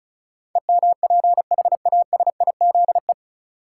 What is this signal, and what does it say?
Morse code, heard as a keyed tone, about 36 wpm, 700 Hz. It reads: EMPHASIZE
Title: Sentences from Top 1500 Words - New Word Drill Alt - 35wpm